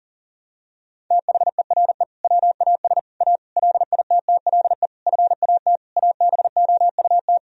Morse code, heard as a keyed tone, about 40 wpm, 700 Hz. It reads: THERE WAS A LITTLE FAT ABOUT
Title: Sentences from Top 1000 Words - Review Part 4 - 40wpm